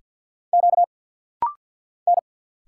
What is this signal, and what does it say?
Morse code, heard as a keyed tone, about 50 wpm, 690 Hz. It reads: Q N
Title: Single Letters - Rapid-Fire 50wpm